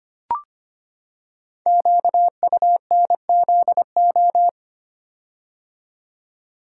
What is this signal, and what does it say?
Morse code, heard as a keyed tone, about 25 wpm, 700 Hz. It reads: QUNZO
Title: Code Groups - 5 Letters - 25wpm